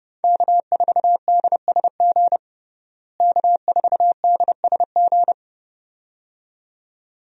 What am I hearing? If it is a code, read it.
Morse code, heard as a keyed tone, about 32 wpm, 700 Hz. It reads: K4DSG K4DSG
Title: USA Call Signs - 5 Characters - Repeated 2x - 30wpm